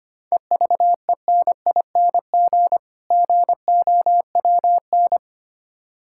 Morse code, heard as a keyed tone, about 25 wpm, 700 Hz. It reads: EVENING GOWN
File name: practice / 25wpm / Difficult Words - Speed-Racing with Hints - 25wpm